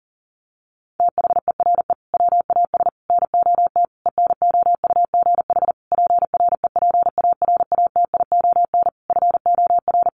Morse code, heard as a keyed tone, about 40 wpm, 700 Hz. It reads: THERE WAS NOT ENOUGH PREPARATION FOR
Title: Sentences from Top 1400 Words 40wpm